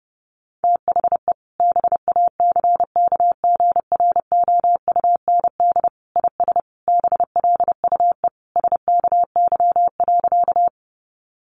Morse code, heard as a keyed tone, about 30 wpm, 700 Hz. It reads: THE BACKGROUND IS BLUE SKY.
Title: Sentences from Top 1300 Words 30wpm